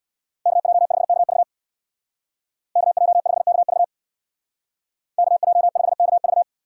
Morse code, heard as a keyed tone, about 61 wpm, 700 Hz. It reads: KJ5KV KJ5KV KJ5KV
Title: Speed Racing - USA 2x2 Call Signs 50wpm